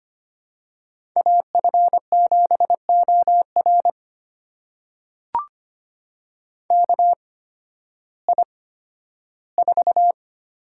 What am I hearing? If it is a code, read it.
Morse code, heard as a keyed tone, about 25 wpm, spaced as for 13 wpm, 700 Hz. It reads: AF7OR KI4
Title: USA Call Signs - 8x Character Spacing - 25wpm